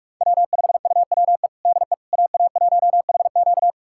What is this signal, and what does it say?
Morse code, heard as a keyed tone, about 46 wpm, 700 Hz. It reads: W5UWE DE AA1HQ
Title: USA Callsign DE Callsign - 5 Characters - 45wpm